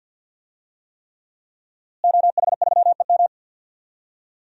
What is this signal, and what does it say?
Morse code, heard as a keyed tone, about 50 wpm, 700 Hz. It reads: OH3EM
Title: DXCC-100 Call Signs - 5 Characters - Repeated 3x - 50wpm